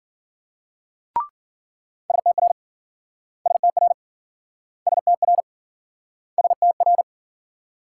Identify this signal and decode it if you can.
Morse code, heard as a keyed tone, about 52 wpm, 700 Hz. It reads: STR STR STR STR